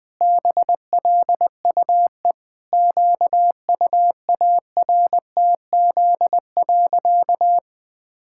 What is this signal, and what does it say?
Morse code, heard as a keyed tone, about 21 wpm, 700 Hz. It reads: BLUE QUARTZ.
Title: Pangram Challenge v2 - 20wpm